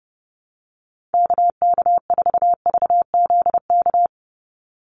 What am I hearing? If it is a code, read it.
Morse code, heard as a keyed tone, about 30 wpm, 700 Hz. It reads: KK4VZK
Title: USA Call Signs 30wpm